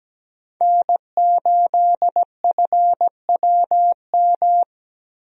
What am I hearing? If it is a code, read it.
Morse code, heard as a keyed tone, about 17 wpm, 700 Hz. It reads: N8FWM
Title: USA Call Signs - 5 Characters - Repeated 2x - 17wpm